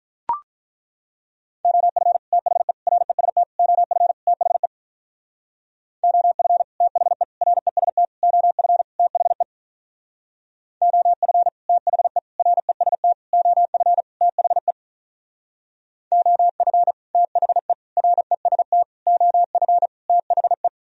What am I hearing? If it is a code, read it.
Morse code, heard as a keyed tone, about 42 wpm, 700 Hz. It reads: OF THE REST OF THE OF THE REST OF THE OF THE REST OF THE OF THE REST OF THE
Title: Speed-Racing - Sets of 5 Words 35wpm